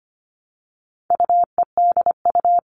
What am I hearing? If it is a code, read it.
Morse code, heard as a keyed tone, about 25 wpm, 700 Hz. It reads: UEDU
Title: All Cut Numbers - 4-Digit 25wpm